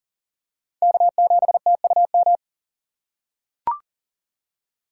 Morse code, heard as a keyed tone, about 40 wpm, 700 Hz. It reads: K7TUM
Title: USA Call Signs - 5 Characters - 3x Character Spacing - 40wpm